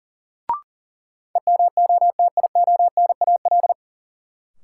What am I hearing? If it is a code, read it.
Morse code, heard as a keyed tone, about 39 wpm, 700 Hz. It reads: EMOTIONAL
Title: Sentences from Top 1000 Words - Review Part 5 - New Word Drill - 40wpm